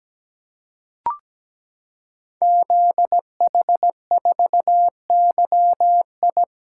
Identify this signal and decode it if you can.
Morse code, heard as a keyed tone, about 17 wpm, 700 Hz. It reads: ZH4YI
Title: Code Groups - 5 Letter:Numbers - 17wpm